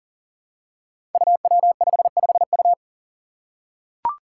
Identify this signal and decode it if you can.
Morse code, heard as a keyed tone, about 40 wpm, 700 Hz. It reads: UW55U
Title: Five Challenging Letter-Number Combinations 40wpm